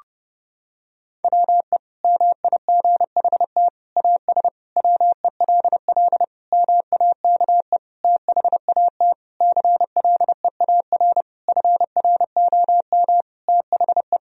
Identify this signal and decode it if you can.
Morse code, heard as a keyed tone, about 32 wpm, 700 Hz. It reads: WE MIGHT AS WELL MAKE THAT CLEAR FROM THE
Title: Sentences from Top 1000 Words - Review Part 1 - 30wpm